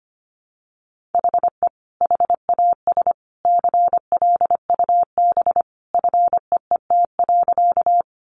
Morse code, heard as a keyed tone, about 25 wpm, 700 Hz. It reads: HE HAS CLUB FEET.
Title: Sentences from Top 1200 Words 25wpm